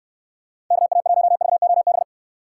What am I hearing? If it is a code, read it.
Morse code, heard as a keyed tone, about 69 wpm, 700 Hz. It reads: KA1VOB